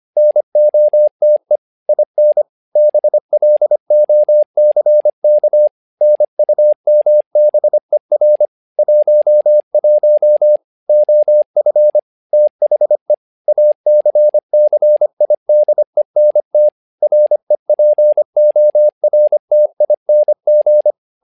Morse code, heard as a keyed tone, about 25 wpm, 600 Hz. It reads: NOTE IN BLOCK NUMBER 11 OF THE ACCIDENT REPORTING
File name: learning center / tower trouble / tower trouble 25wpm